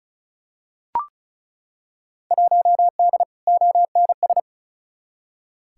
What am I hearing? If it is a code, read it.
Morse code, heard as a keyed tone, about 36 wpm, 700 Hz. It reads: 1D ONS
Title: Field Day Exchanges - 35wpm